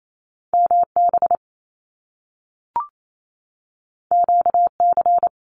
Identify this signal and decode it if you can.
Morse code, heard as a keyed tone, about 28 wpm, 700 Hz. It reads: MB QC
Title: Canadian Provinces and Territories 28wpm